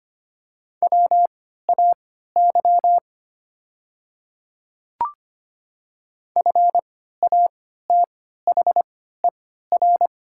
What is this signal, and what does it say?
Morse code, heard as a keyed tone, about 25 wpm, spaced as for 14 wpm, 690 Hz. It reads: WAY FATHER
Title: 3x Character Spacing - Top 300 Words - 25wpm